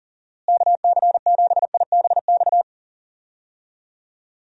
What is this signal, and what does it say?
Morse code, heard as a keyed tone, about 40 wpm, 700 Hz. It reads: KC7IBX
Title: USA Call Signs - 6 Characters - Repeated 3x - 40wpm